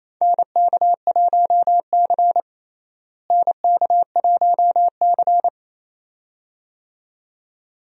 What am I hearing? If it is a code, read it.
Morse code, heard as a keyed tone, about 28 wpm, 700 Hz. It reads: NK1C NK1C
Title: USA Call Signs - 4 Characters - Repeated 2x - 28wpm